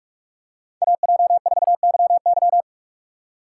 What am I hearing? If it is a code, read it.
Morse code, heard as a keyed tone, about 46 wpm, 700 Hz. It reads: AJ4YY
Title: USA Call Signs - 5 Characters - 3x Character Spacing - 45wpm